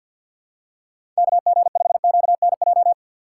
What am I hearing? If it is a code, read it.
Morse code, heard as a keyed tone, about 50 wpm, 700 Hz. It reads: KG5QNJ
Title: USA Call Signs - 4x Character Spacing - 50wpm